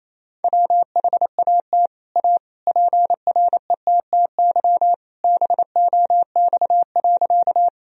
Morse code, heard as a keyed tone, about 28 wpm, 700 Hz. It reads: WHAT A PRETTY BOX.
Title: Sentences from Top 1000 Words - Review Part 2 - 28wpm